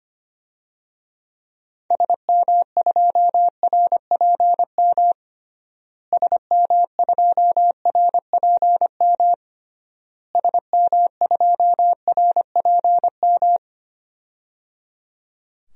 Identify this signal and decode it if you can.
Morse code, heard as a keyed tone, about 25 wpm, 700 Hz. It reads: SM2RPM SM2RPM SM2RPM
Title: DXCC-100 Call Signs - 6 Characters - Repeated 3x - 25wpm